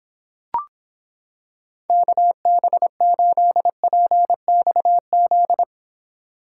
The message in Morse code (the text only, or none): KB8PXZ